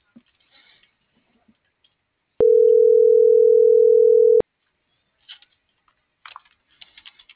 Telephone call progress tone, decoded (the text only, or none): ringback tone